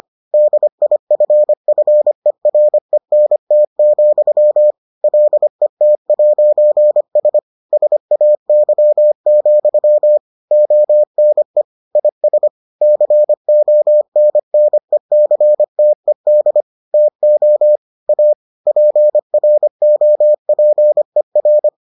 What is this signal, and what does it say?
Morse code, heard as a keyed tone, about 25 wpm, 600 Hz. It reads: DIFFERENT, LET'S SAY, ONE IS CONNECTED TO A PROPER